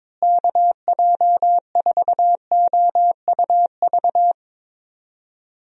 Morse code, heard as a keyed tone, about 22 wpm, 700 Hz. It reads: KJ4OUV
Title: USA Call Signs - 6 Characters - Repeated 3x - 22wpm